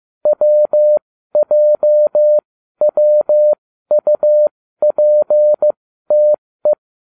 Morse code, heard as a keyed tone, about 15 wpm, spaced as for 12 wpm, 600 Hz. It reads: WJWUPTE